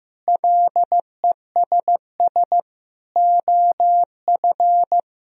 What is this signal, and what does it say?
Morse code, heard as a keyed tone, about 15 wpm, 700 Hz. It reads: LESS OF
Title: Sentences from Top 1000 Words - Review Part 2 - 15wpm